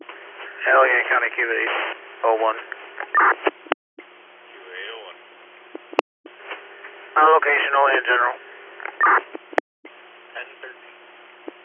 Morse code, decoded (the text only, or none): none